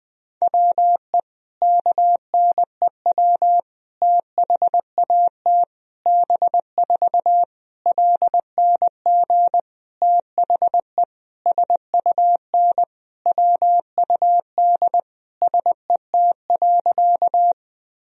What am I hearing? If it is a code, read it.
Morse code, heard as a keyed tone, about 20 wpm, 700 Hz. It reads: WE KNEW THAT B4 LNG THE SUN WUD SET.
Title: Sentences from Top 400 Words and Abbreviations 20wpm